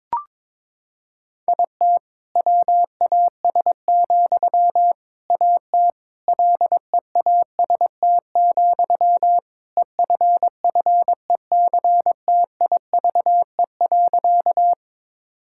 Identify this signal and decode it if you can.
Morse code, heard as a keyed tone, about 22 wpm, 700 Hz. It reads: IT WAS, AT LEAST, EFFECTIVE.
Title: Sentences from Top 1000 Words - Review Part 6 - 22wpm